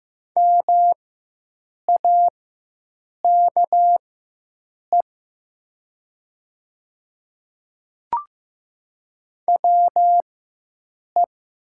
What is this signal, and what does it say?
Morse code, heard as a keyed tone, about 15 wpm, spaced as for 7 wpm, 700 Hz. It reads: MAKE WE
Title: 4x Character Spacing - Top 100 Words - 15wpm